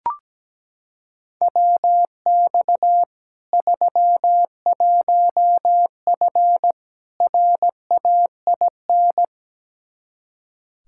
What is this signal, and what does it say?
Morse code, heard as a keyed tone, about 17 wpm, 700 Hz. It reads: WX 31F RAIN
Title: Practice QSO - Exchange 2 - Info - 17wpm